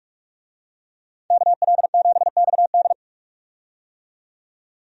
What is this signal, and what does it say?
Morse code, heard as a keyed tone, about 46 wpm, 700 Hz. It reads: KL7XD
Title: DXCC-100 Call Signs - 5 Characters - 45wpm